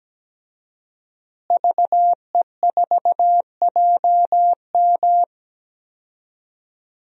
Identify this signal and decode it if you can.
Morse code, heard as a keyed tone, about 17 wpm, 700 Hz. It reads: VE4JM
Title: DXCC-100 Call Signs - 5 Characters - Repeated 3x - 17wpm